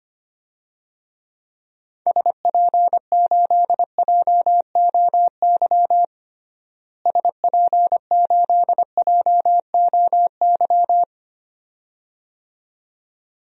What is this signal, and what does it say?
Morse code, heard as a keyed tone, about 25 wpm, 700 Hz. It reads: SP8JOY SP8JOY
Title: DXCC-100 Call Signs - 6 Characters - Repeated 2x - 25wpm